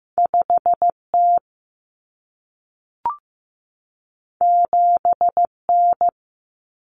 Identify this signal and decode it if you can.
Morse code, heard as a keyed tone, about 15 wpm, 700 Hz. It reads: 5T 7N
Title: Common Cut Numbers - 2-Digit 15wpm